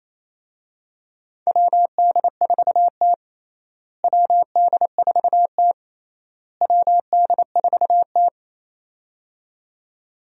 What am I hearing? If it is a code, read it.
Morse code, heard as a keyed tone, about 28 wpm, 700 Hz. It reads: WD4T WD4T WD4T
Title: USA Call Signs - 4 Characters - Repeated 3x - 28wpm